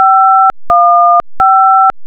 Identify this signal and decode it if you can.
DTMF, heard as telephone touch tones, tones of about 530 ms. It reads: 515